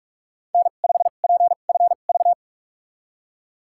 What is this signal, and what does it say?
Morse code, heard as a keyed tone, about 44 wpm, spaced as for 32 wpm, 700 Hz. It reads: N5PFV
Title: USA Call Signs - 2x Character Spacing - 45wpm